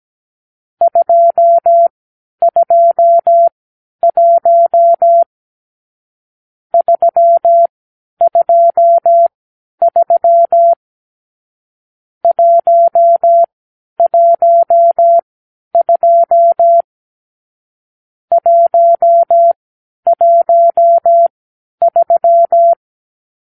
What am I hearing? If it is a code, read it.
Morse code, heard as a keyed tone, about 17 wpm, spaced as for 10 wpm, 680 Hz. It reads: 221 323 112 113
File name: k7qo / Lesson 44 (1-3)